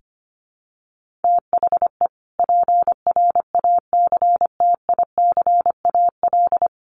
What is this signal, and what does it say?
Morse code, heard as a keyed tone, about 24 wpm, 700 Hz. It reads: THE PRACTICAL
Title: Sentences from Top 1400 Words 25wpm